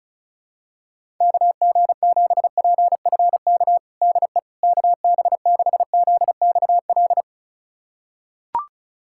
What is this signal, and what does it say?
Morse code, heard as a keyed tone, about 36 wpm, 700 Hz. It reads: KG7PFK DE KB6ZXL
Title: USA Callsign DE Callsign - 6 Characters - 35wpm